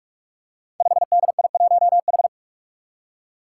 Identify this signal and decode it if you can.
Morse code, heard as a keyed tone, about 46 wpm, 700 Hz. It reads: 5DI1H
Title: Five Challenging Letter-Number Combinations 45wpm